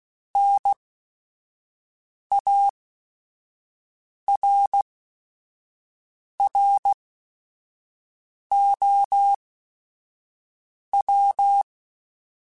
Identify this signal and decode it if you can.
Morse code, heard as a keyed tone, about 16 wpm, spaced as for 5 wpm, 800 Hz. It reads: NARROW
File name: Chuckles / Chuckles-mp3s / MiddleAge @5